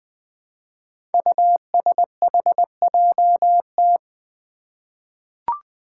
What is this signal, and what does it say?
Morse code, heard as a keyed tone, about 20 wpm, 700 Hz. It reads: USHJT
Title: Five Challenging Letter-Number Combinations 20wpm